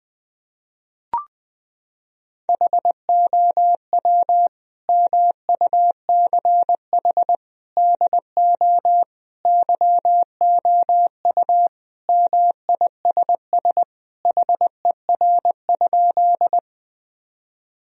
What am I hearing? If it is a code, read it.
Morse code, heard as a keyed tone, about 19 wpm, 700 Hz. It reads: HOW MUCH DO YOU MISS HER?
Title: Sentences from Top 1000 Words - Review Part 3 - 20wpm